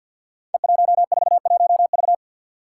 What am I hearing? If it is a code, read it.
Morse code, heard as a keyed tone, about 46 wpm, 700 Hz. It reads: E141V
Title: Five Challenging Letter-Number Combinations 50wpm